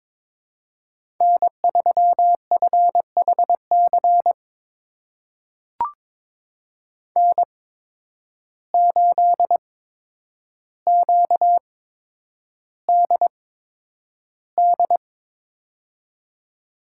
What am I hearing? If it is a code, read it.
Morse code, heard as a keyed tone, about 22 wpm, spaced as for 9 wpm, 700 Hz. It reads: N3FHC N8QDD
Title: USA Call Signs - 8x Character Spacing - 22wpm